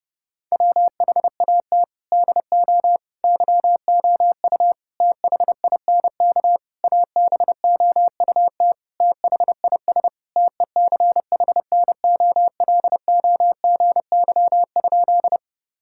Morse code, heard as a keyed tone, about 30 wpm, 700 Hz. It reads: WHAT DO YOU THINK ABOUT THIS TECHNOLOGY?
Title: Sentences from Top 1000 Words - Review Part 5 - 30wpm